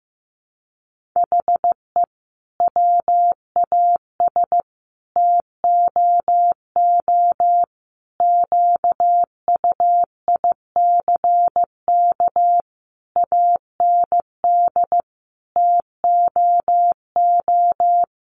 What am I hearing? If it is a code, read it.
Morse code, heard as a keyed tone, about 15 wpm, 700 Hz. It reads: HE WAS TOO QUICK AND TOO
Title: Sentences from Top 600 Words 15wpm